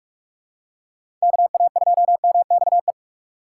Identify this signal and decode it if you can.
Morse code, heard as a keyed tone, about 46 wpm, 700 Hz. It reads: KA2MXE
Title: USA Call Signs - 6x Character Spacing - 45wpm